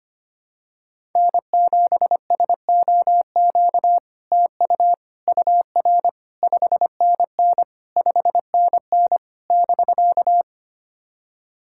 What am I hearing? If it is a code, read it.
Morse code, heard as a keyed tone, about 25 wpm, 700 Hz. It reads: N7SOQ TU UR 5NN 5NN <BK>
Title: SOTA QSO Practice 25wpm